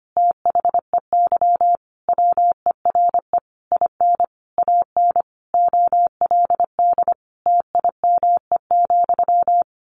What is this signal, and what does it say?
Morse code, heard as a keyed tone, about 25 wpm, 700 Hz. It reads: THEY WERE IN AN OLD TIME,
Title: A Sound Of Thunder 25wpm